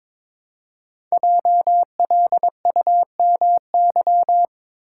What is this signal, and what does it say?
Morse code, heard as a keyed tone, about 22 wpm, 690 Hz. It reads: JLUMY
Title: Code Groups - 5 Letters - 22wpm